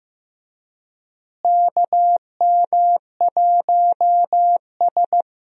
Morse code, heard as a keyed tone, about 16 wpm, 700 Hz. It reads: KM1S
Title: USA Call Signs - 4 Characters - Repeated 3x - 15wpm